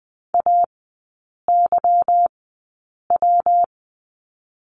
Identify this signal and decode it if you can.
Morse code, heard as a keyed tone, about 20 wpm, 700 Hz. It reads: A Y W
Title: 145 Lesson 24e - Letter y - Three Characters 20wpm